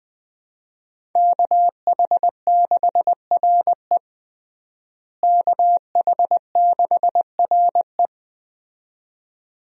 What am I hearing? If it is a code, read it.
Morse code, heard as a keyed tone, about 21 wpm, 700 Hz. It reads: KH6RE KH6RE
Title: USA Call Signs - 5 Characters - Repeated 2x - 20wpm